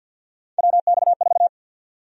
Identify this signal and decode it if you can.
Morse code, heard as a keyed tone, about 51 wpm, 700 Hz. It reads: WX4
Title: USA Call Sign Prefixes - 50wpm